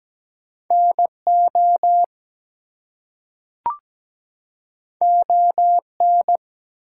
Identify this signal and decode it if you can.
Morse code, heard as a keyed tone, about 17 wpm, 700 Hz. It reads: NO ON
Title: Two Letter Words - 17wpm